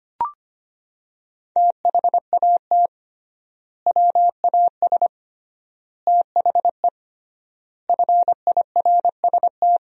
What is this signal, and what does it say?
Morse code, heard as a keyed tone, about 25 wpm, 700 Hz. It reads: THAT WAS THE FIRST
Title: 3x Word-Spacing - Sentences from Top 300 Words - 25wpm